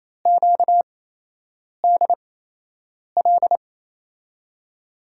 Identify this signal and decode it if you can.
Morse code, heard as a keyed tone, about 28 wpm, spaced as for 8 wpm, 700 Hz. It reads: QDL
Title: USA Call Signs - 8x Character Spacing - 28wpm